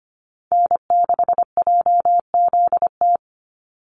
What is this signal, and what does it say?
Morse code, heard as a keyed tone, about 25 wpm, 700 Hz. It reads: N6JZT
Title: NA SOTA Activator Callsigns 25wpm